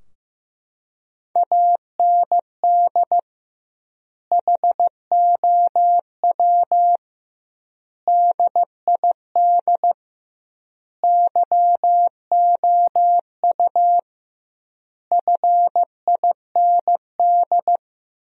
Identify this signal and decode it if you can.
Morse code, heard as a keyed tone, about 15 wpm, 700 Hz. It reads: AND HOW DID YOU FIND
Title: 2x Word-Spacing - Sentences from Top 100 Words - 15wpm